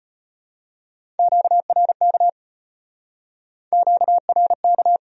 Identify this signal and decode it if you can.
Morse code, heard as a keyed tone, about 35 wpm, 700 Hz. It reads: QRK QRK